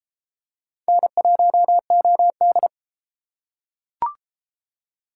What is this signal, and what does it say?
Morse code, heard as a keyed tone, about 31 wpm, 700 Hz. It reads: N1OD